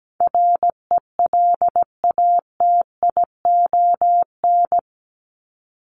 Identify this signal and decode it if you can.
Morse code, heard as a keyed tone, about 17 wpm, 700 Hz. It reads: RELATION
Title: ES:S Suffix - Top 100 Words with Base Words 17wpm